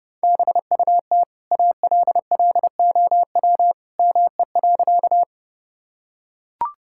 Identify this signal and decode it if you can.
Morse code, heard as a keyed tone, about 32 wpm, 700 Hz. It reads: BUT ALLOW ME.
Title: Sentences from Top 1000 Words - Review Part 5 - 30wpm